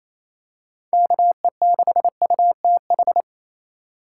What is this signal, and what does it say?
Morse code, heard as a keyed tone, about 28 wpm, 700 Hz. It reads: KE6UTH
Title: USA Call Signs - 6 Characters - 3x Character Spacing - 28wpm